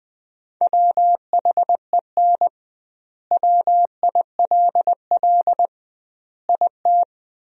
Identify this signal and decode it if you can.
Morse code, heard as a keyed tone, about 21 wpm, 700 Hz. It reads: WHEN WILL IT
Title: 2x Word-Spacing - Sentences from Top 200 Words - 20wpm